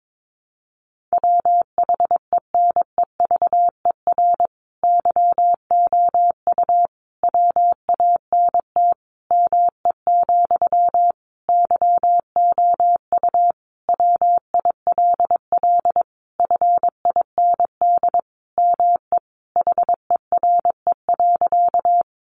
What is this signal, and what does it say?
Morse code, heard as a keyed tone, about 22 wpm, 700 Hz. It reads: WHENEVER YOU WANT ME, YOU WILL FIND ME HERE.
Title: Sentences from Top 1200 Words 22wpm